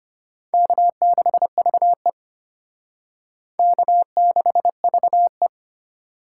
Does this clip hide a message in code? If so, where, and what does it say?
Yes: Morse code from 0.5 to 5.5 s, heard as a keyed tone, about 32 wpm, 700 Hz. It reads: K6VE K6VE